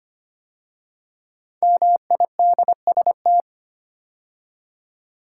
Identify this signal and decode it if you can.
Morse code, heard as a keyed tone, about 25 wpm, 700 Hz. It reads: MIDST
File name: 5 Letter Words - First 3 Letters Given - 25wpm